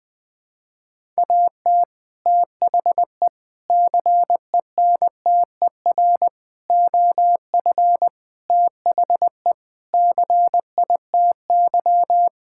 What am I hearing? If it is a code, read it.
Morse code, heard as a keyed tone, about 20 wpm, 700 Hz. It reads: AT THE CENTER OF THE CITY
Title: Sentences with 4-Word Phrases - 20wpm